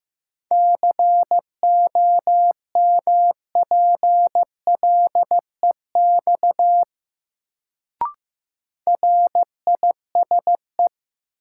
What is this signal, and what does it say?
Morse code, heard as a keyed tone, about 15 wpm, 700 Hz. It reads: COMPLEX RISE
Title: Sentences from Top 900 Words - New Word Drill - 15wpm